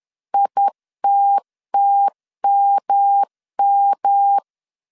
Morse code, heard as a keyed tone, about 10 wpm, 800 Hz. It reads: ITTMM
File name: CW / DET / 1 3 1500C S C 10wpm 3335